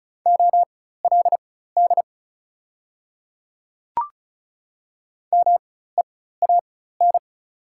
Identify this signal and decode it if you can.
Morse code, heard as a keyed tone, about 36 wpm, spaced as for 16 wpm, 700 Hz. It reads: OLD MEAN
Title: 4x Character Spacing - Top 300 Words - 35wpm